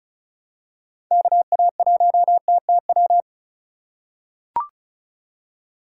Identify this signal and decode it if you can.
Morse code, heard as a keyed tone, about 36 wpm, 700 Hz. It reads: KA1TTW
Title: USA Call Signs - 6 Characters - 4x Character Spacing - 35wpm